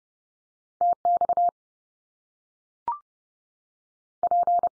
Morse code, heard as a keyed tone, about 30 wpm, 700 Hz. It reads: TX P
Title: US States 30WPM